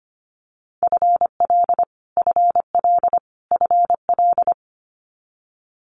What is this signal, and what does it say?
Morse code, heard as a keyed tone, about 25 wpm, 700 Hz. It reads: FL FL FL
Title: Multi-letter Phonemes Repeated 3x 25wpm